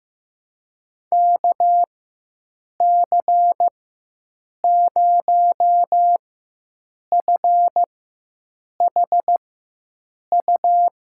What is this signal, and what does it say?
Morse code, heard as a keyed tone, about 15 wpm, spaced as for 7 wpm, 700 Hz. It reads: KC0FHU
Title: USA Call Signs - 6 Characters - 4x Character Spacing - 15wpm